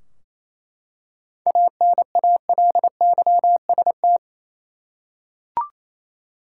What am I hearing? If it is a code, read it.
Morse code, heard as a keyed tone, about 27 wpm, 700 Hz. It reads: ANALYST
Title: Practice QSO - Exchange 3 - More Info - 28wpm